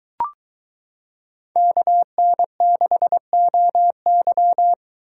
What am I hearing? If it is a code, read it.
Morse code, heard as a keyed tone, about 22 wpm, 700 Hz. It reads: KN6OY